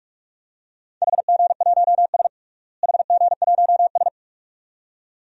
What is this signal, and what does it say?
Morse code, heard as a keyed tone, about 46 wpm, 700 Hz. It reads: HG1S HG1S
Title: DXCC-100 Call Signs - 4 Characters - Repeated 2x - 45wpm